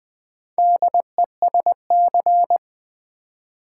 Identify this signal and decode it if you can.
Morse code, heard as a keyed tone, about 20 wpm, 700 Hz. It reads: DESC
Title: Build A Word - 2 Letters At A Time - 8-Letter Words - 20wpm